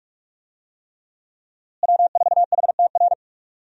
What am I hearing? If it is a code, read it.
Morse code, heard as a keyed tone, about 42 wpm, 700 Hz. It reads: W4HTR
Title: USA Call Signs - 5 Characters - Repeated 3x - 45wpm